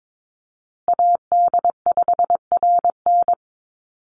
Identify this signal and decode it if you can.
Morse code, heard as a keyed tone, about 22 wpm, 700 Hz. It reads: AD5RN
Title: USA Call Signs Repeated 2x 22wpm